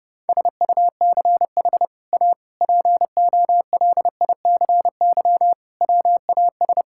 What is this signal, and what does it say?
Morse code, heard as a keyed tone, about 30 wpm, 700 Hz. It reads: SUCH A POLICY WAS
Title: Sentences from Top 1000 Words - Review Part 5 - 30wpm